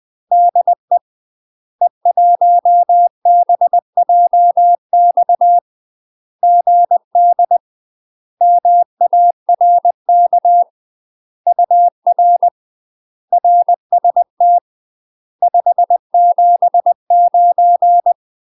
Morse code, heard as a keyed tone, about 20 wpm, 700 Hz. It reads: DE E1BJX GD MARK UR RST 579
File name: UK MASTER QSOs TRAINING / 20WPM / UK MASTER QSOs 0036